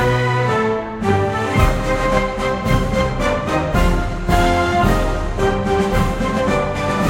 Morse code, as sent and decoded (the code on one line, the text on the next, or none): none